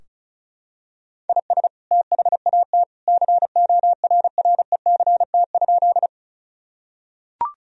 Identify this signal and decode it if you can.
Morse code, heard as a keyed tone, about 36 wpm, 700 Hz. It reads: IS THAT CORRECT?
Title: Sentences from Top 1000 Words - Review Part 2 - 35wpm